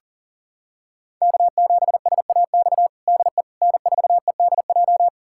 Morse code, heard as a keyed tone, about 40 wpm, 700 Hz. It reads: K7SAX DE N4EDJ
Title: USA Callsign DE Callsign - 5 Characters - 40wpm